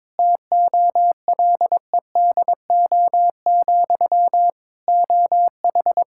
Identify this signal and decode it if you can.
Morse code, heard as a keyed tone, about 22 wpm, 700 Hz. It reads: TOLEDO, OH
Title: Practice QSO - Exchange 1 - Beginning - 22wpm